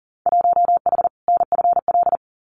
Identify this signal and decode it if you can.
Morse code, heard as a keyed tone, about 40 wpm, 700 Hz. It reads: 1H NFL
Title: Winter Field Day Exchanges 40wpm